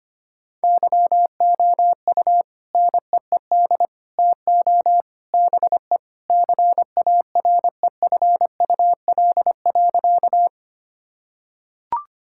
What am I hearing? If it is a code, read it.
Morse code, heard as a keyed tone, about 25 wpm, 700 Hz. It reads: YOU NEED TO BE CAREFUL.
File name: practice / 25wpm / Sentences with 3-Word Phrases - 25wpm